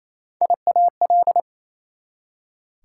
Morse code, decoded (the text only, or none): IAL